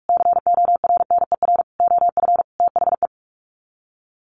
Morse code, heard as a keyed tone, about 44 wpm, 700 Hz. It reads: CORNER OF THE
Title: Sets of 5 Words Encore2 45wpm